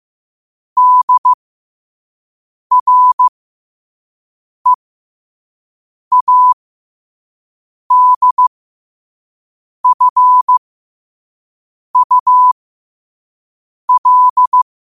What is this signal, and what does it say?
Morse code, heard as a keyed tone, about 15 wpm, spaced as for 5 wpm, 1000 Hz. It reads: DREADFUL
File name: qotdfiles / 5wpm